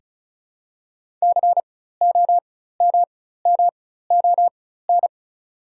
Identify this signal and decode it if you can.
Morse code, heard as a keyed tone, about 36 wpm, spaced as for 16 wpm, 700 Hz. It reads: COMMON
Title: 4x Character Spacing - Top 500 Words - 35wpm